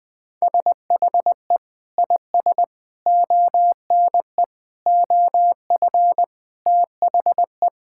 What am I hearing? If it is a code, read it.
Morse code, heard as a keyed tone, about 20 wpm, 700 Hz. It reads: SHE IS ONE OF THE
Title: Sentences with 2-Word Phrases - 20wpm